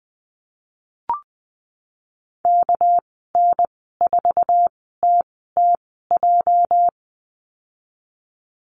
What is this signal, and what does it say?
Morse code, heard as a keyed tone, about 20 wpm, spaced as for 14 wpm, 700 Hz. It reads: KN4TTJ
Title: 245 Lesson 39g - Letter z - Call Signs 20wpm